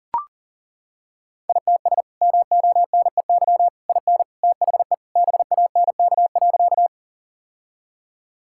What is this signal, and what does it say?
Morse code, heard as a keyed tone, about 40 wpm, 700 Hz. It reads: ITS MONEY IN THE BANK.
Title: Sentences from Top 1000 Words - Review Part 6 - 40wpm